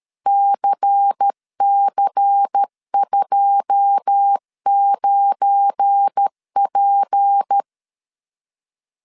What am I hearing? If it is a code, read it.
Morse code, heard as a keyed tone, about 13 wpm, 800 Hz. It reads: CC29P
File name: LEZIONI / 10 SIGNS / 10C DOT 12wpm 2451